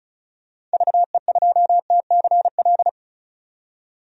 Morse code, heard as a keyed tone, about 36 wpm, 700 Hz. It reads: VE2TCL
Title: DXCC-100 Call Signs - 6 Characters - 35wpm